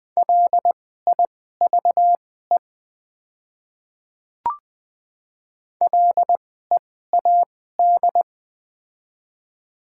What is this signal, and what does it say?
Morse code, heard as a keyed tone, about 20 wpm, spaced as for 14 wpm, 700 Hz. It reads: LIVE LEAD
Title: 2x Character Spacing - Top 500 Words - 20wpm